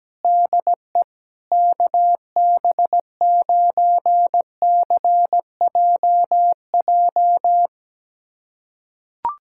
Morse code, heard as a keyed tone, about 17 wpm, 700 Hz. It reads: DE KB9CJJ
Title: USA Callsign DE Callsign - 6 Characters - 17wpm